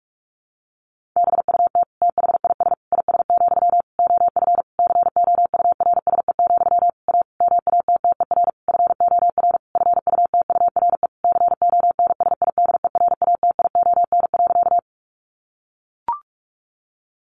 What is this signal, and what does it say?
Morse code, heard as a keyed tone, about 44 wpm, 700 Hz. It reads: BUT THIS IS, OF COURSE, A MATTER FOR FUTURE CONSIDERATION.
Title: Sentences from Top 1500 Words 45wpm